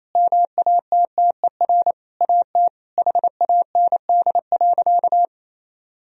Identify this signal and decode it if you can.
Morse code, heard as a keyed tone, about 28 wpm, 700 Hz. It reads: MATTER AT HAND.
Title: Sentences from Top 1000 Words - Review Part 3 - 28wpm